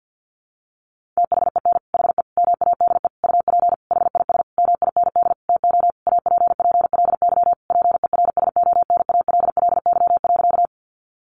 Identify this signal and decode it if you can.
Morse code, heard as a keyed tone, about 50 wpm, 700 Hz. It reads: THEN HE MADE UP HIS MIND TO APPLY PERSONALLY.
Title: Sentences from Top 1400 Words 50wpm